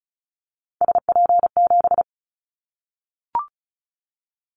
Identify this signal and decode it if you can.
Morse code, heard as a keyed tone, about 34 wpm, 700 Hz. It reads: SP7
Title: Callsign Prefixes for Top 25 DXCC - Repeated 3x - 35wpm